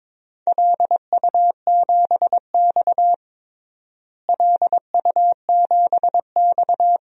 Morse code, heard as a keyed tone, about 21 wpm, 700 Hz. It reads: LU7X LU7X
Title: DXCC-100 Call Signs - 4 Characters - Repeated 3x - 22wpm